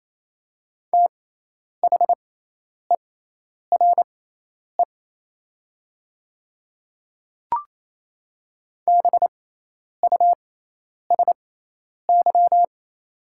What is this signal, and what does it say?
Morse code, heard as a keyed tone, about 28 wpm, spaced as for 10 wpm, 700 Hz. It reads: THERE BUSY
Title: 6x Character Spacing - Top 500 Words - 28wpm